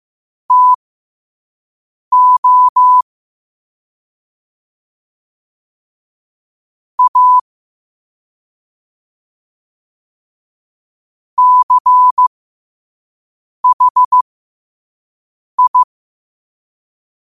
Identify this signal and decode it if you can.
Morse code, heard as a keyed tone, about 15 wpm, spaced as for 5 wpm, 1000 Hz. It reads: TO A CHI
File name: qotdfiles / 5wpm